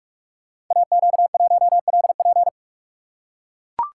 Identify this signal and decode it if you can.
Morse code, heard as a keyed tone, about 44 wpm, 700 Hz. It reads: AQ1LP
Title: Code Groups - 5 Letter:Numbers - 45wpm